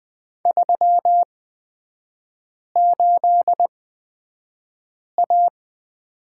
Morse code, heard as a keyed tone, about 20 wpm, spaced as for 5 wpm, 700 Hz. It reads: 38A
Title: 229 Lesson 37e - Letter x - Three Characters 20wpm